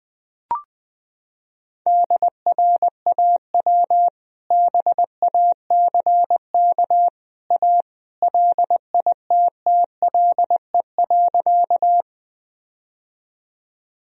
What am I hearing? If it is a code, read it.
Morse code, heard as a keyed tone, about 20 wpm, 700 Hz. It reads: DRAW BACK A LITTLE.
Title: Sentences from Top 1000 Words - Review Part 1 - 20wpm